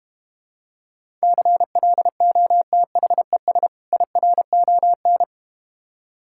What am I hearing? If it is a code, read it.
Morse code, heard as a keyed tone, about 32 wpm, 700 Hz. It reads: CLOTHES IRON